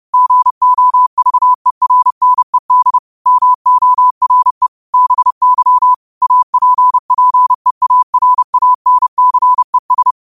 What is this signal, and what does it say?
Morse code, heard as a keyed tone, about 30 wpm, 1000 Hz. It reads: GOVERNED MORE BY APPEARANCES
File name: qotdfiles / 30wpm